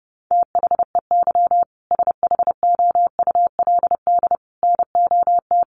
Morse code, heard as a keyed tone, about 30 wpm, 700 Hz. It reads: THEY SHOULD NOT
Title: Sentences from Top 1500 Words 30wpm